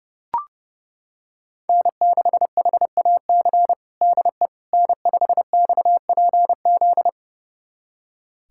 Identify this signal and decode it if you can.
Morse code, heard as a keyed tone, about 32 wpm, 700 Hz. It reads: N6HAC DE N5XPZ
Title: USA Callsign DE Callsign - 5 Characters - 30wpm